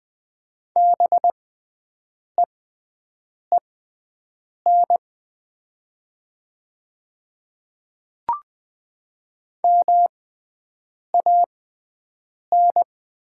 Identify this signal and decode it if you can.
Morse code, heard as a keyed tone, about 20 wpm, spaced as for 7 wpm, 700 Hz. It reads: BEEN MAN
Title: 6x Character Spacing - Top 200 Words - 20wpm